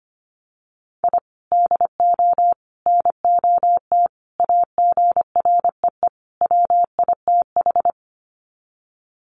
Morse code, heard as a keyed tone, about 25 wpm, 700 Hz. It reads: I DO NOT AGREE WITH
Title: Sets of 5 Words Encore 25wpm